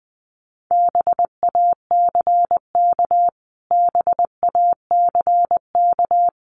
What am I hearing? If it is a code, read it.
Morse code, heard as a keyed tone, about 19 wpm, 700 Hz. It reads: BACK BACK
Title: Top 200 Words Repeated 20wpm